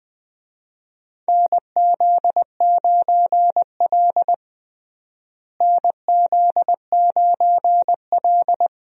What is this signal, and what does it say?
Morse code, heard as a keyed tone, about 20 wpm, 700 Hz. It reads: NZ9L NZ9L
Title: USA Call Signs - 4 Characters - Repeated 3x - 20wpm